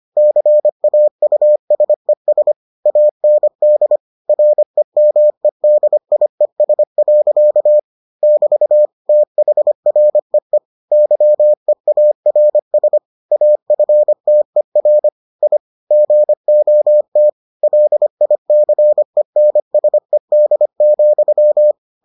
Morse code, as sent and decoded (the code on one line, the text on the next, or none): -.-. .- ..- ... ... / .- -. -.. / .-. -- . -.. .. ... .-.-.- / -...- / - .... .-. . / -.-- . .- .-. ... / .- ..-. - . .-. / .. / --. --- - / .-.. .. -.-. -. ... -.. --..--
CAUSES AND REMEDIES. <BT> THREE YEARS AFTER I GOT LICENSED,